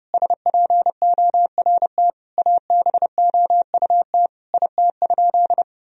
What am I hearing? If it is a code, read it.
Morse code, heard as a keyed tone, about 30 wpm, 700 Hz. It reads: SPORT ABOUT IT?
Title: Sentences from Top 1000 Words - Review Part 4 - 30wpm